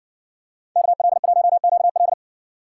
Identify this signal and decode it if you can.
Morse code, heard as a keyed tone, about 60 wpm, 700 Hz. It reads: KL1QL